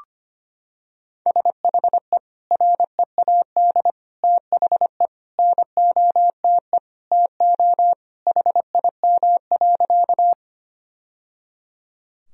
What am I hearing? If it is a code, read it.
Morse code, heard as a keyed tone, about 25 wpm, 700 Hz. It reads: SHE READ THE NOTE TO HIM.
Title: Sentences from Top 1000 Words - Review Part 2 - 25wpm